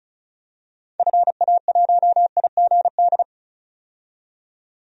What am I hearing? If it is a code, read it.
Morse code, heard as a keyed tone, about 34 wpm, 700 Hz. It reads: FA1IGD
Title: DXCC-100 Call Signs - 6 Characters - 35wpm